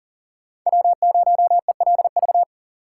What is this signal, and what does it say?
Morse code, heard as a keyed tone, about 39 wpm, 690 Hz. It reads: W0ELV
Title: Code Groups - 5 Letter:Numbers - 40wpm